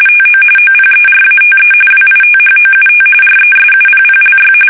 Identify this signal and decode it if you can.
RTTY, about 45.45 baud. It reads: TO OPERATE WITHOUT RESUPPLY
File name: fsk850 / battleshipnj